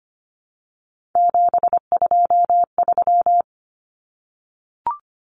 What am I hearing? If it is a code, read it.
Morse code, heard as a keyed tone, about 25 wpm, 700 Hz. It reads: 723
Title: Numbers - 3-Digit 25wpm